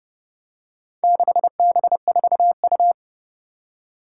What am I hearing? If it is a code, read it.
Morse code, heard as a keyed tone, about 30 wpm, 700 Hz. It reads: 6B4U
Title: Four Challenging Letter-Number Combinations 30wpm